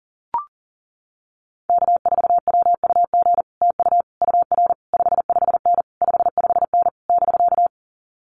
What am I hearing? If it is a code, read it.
Morse code, heard as a keyed tone, about 40 wpm, 700 Hz. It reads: K4WUG TU UR 55N 55N <BK>
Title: Common QSO Phrases Encore 40wpm